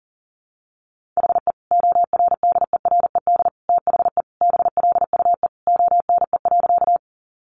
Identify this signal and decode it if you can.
Morse code, heard as a keyed tone, about 37 wpm, 700 Hz. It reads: HE ORDERED THE BLUE ONE.
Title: Additional Easy Phrases 40wpm